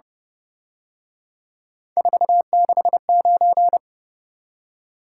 Morse code, heard as a keyed tone, about 30 wpm, 700 Hz. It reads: 469